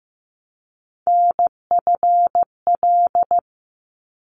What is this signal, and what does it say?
Morse code, heard as a keyed tone, about 15 wpm, 700 Hz. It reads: NFL
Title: ARRL Sections 15wpm